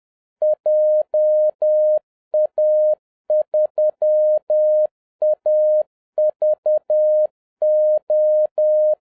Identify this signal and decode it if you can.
Morse code, heard as a keyed tone, about 10 wpm, 610 Hz. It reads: JA3AVO